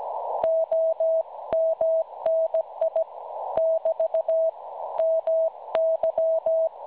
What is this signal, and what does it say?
Morse code, heard as a keyed tone, about 17 wpm, 650 Hz. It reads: OMNI <BT> MY
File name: cwronde / websdr recording .6kHz